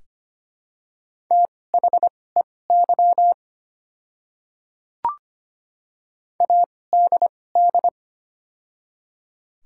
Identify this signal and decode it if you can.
Morse code, heard as a keyed tone, about 25 wpm, spaced as for 18 wpm, 700 Hz. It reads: THEY ADD
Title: 2x Character Spacing - Top 300 Words - 25wpm